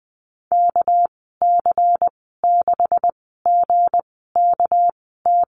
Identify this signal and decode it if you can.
Morse code, heard as a keyed tone, about 20 wpm, spaced as for 15 wpm, 700 Hz. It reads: KC6GKT
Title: 207 Lesson 33g - Letter j - Call Signs 20wpm